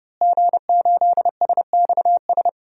Morse code, heard as a keyed tone, about 30 wpm, 700 Hz. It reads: G8SXS
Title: DXCC-100 Call Signs - 5 Characters - Repeated 3x - 30wpm